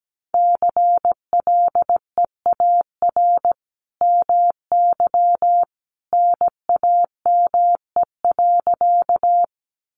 Morse code, heard as a keyed tone, about 17 wpm, 700 Hz. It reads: CLEAR MY NAME.
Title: Sentences from Top 500 Words 17wpm